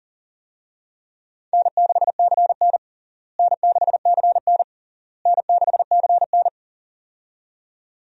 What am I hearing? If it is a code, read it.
Morse code, heard as a keyed tone, about 37 wpm, 700 Hz. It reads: N6CN N6CN N6CN
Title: USA Call Signs - 4 Characters - Repeated 3x - 40wpm